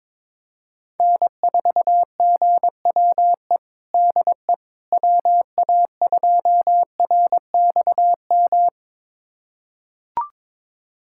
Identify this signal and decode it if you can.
Morse code, heard as a keyed tone, about 22 wpm, 700 Hz. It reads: N4GWE DE WA2RXM
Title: USA Callsign DE Callsign 22wpm